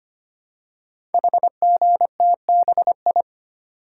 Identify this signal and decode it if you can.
Morse code, heard as a keyed tone, about 25 wpm, 700 Hz. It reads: HGTBI
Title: Code Groups - 5 Letters - 25wpm